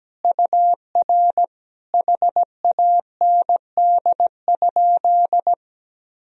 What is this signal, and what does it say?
Morse code, heard as a keyed tone, about 17 wpm, 700 Hz. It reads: UR HAND?
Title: Sentences from Top 200 Words and Abbreviations 17wpm